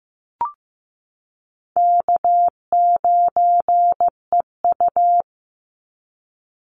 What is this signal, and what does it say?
Morse code, heard as a keyed tone, about 15 wpm, 700 Hz. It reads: K9EU
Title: Easy Call Signs Repeated 2x - 15wpm